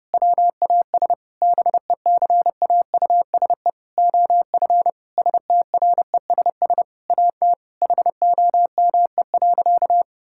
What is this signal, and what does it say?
Morse code, heard as a keyed tone, about 30 wpm, 700 Hz. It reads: WAS BECAUSE OF STRESS AT HOME.
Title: Sentences from Top 1000 Words - Review Part 6 - 30wpm